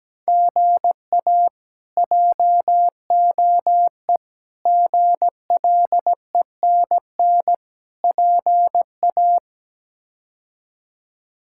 Morse code, heard as a keyed tone, about 17 wpm, 700 Hz. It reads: GA JOE GLENN PA
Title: K1USN SST QSO Practice - 17wpm